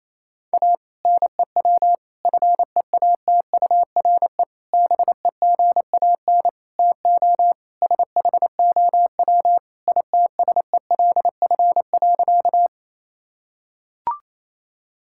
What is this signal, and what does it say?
Morse code, heard as a keyed tone, about 28 wpm, 700 Hz. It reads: A NEW FEATURE BEGAN TO SHOW ITSELF.
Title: Sentences from Top 1000 Words - Review Part 5 - 28wpm